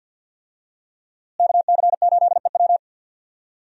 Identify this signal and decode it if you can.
Morse code, heard as a keyed tone, about 48 wpm, 700 Hz. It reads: KC8EW
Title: USA Call Signs - 5 Characters - Repeated 3x - 50wpm